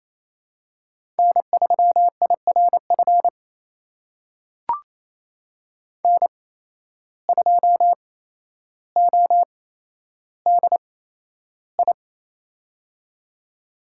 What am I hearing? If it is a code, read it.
Morse code, heard as a keyed tone, about 28 wpm, spaced as for 12 wpm, 700 Hz. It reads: N3IRF N2ODI
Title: USA Call Signs - 5 Characters - 8x Character Spacing - 28wpm